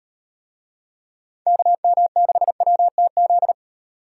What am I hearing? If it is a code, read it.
Morse code, heard as a keyed tone, about 36 wpm, 700 Hz. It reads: KM6WTZ